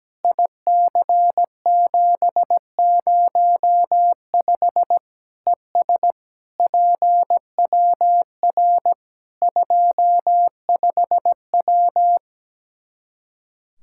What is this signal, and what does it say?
Morse code, heard as a keyed tone, about 17 wpm, 700 Hz. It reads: IC705 ES PWR 25W
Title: Practice QSO - Exchange 2 - Info - 17wpm